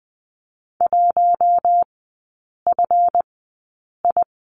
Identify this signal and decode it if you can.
Morse code, heard as a keyed tone, about 20 wpm, spaced as for 8 wpm, 700 Hz. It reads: 1FI
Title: 138 Lesson 23e - Letter f - Three Characters 20wpm